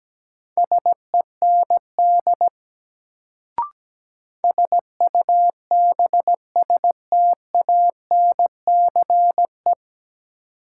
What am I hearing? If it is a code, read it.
Morse code, heard as a keyed tone, about 17 wpm, 700 Hz. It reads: SEND SUBSTANCE
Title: Sentences from Top 1500 Words - New Word Drill Alt - 17wpm